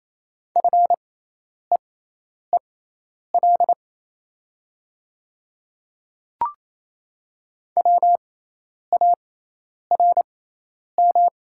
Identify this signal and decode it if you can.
Morse code, heard as a keyed tone, about 28 wpm, spaced as for 10 wpm, 700 Hz. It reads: FEEL WARM
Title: 6x Character Spacing - Top 500 Words - 28wpm